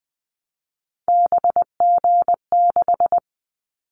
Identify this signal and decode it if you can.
Morse code, heard as a keyed tone, about 20 wpm, 700 Hz. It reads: BG6
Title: Callsign Prefixes for Top 100 DXCC - 20wpm